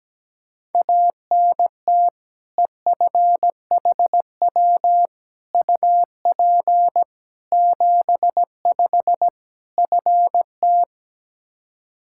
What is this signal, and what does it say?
Morse code, heard as a keyed tone, about 17 wpm, 700 Hz. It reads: ANT EFHW UP 75 FT
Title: Practice QSO - Exchange 2 - Info - 17wpm